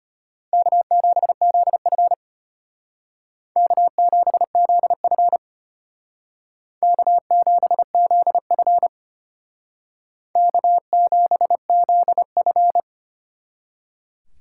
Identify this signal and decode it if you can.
Morse code, heard as a keyed tone, about 32 wpm, 700 Hz. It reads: K7ZF K7ZF K7ZF K7ZF